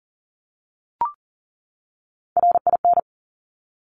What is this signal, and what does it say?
Morse code, heard as a keyed tone, about 38 wpm, 690 Hz. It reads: RIN
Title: Top Three Letter Combinations in Words 40wpm